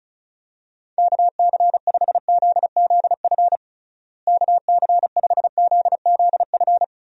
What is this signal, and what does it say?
Morse code, heard as a keyed tone, about 34 wpm, 700 Hz. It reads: KC5ZZF KC5ZZF
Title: USA Call Signs - 6 Characters - Repeated 3x - 35wpm